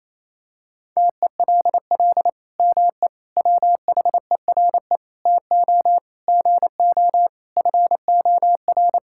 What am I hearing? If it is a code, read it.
Morse code, heard as a keyed tone, about 27 wpm, 700 Hz. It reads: TELL ME WHERE TO GO FOR
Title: Sentences from Top 1600 Words 28wpm